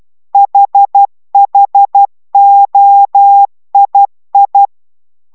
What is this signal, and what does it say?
Morse code, heard as a keyed tone, about 12 wpm, 800 Hz. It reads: HHOII